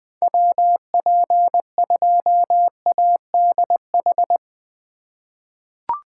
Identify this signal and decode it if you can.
Morse code, heard as a keyed tone, about 21 wpm, 700 Hz. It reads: WP2ADH
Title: USA Call Signs - 6 Characters - 3x Character Spacing - 20wpm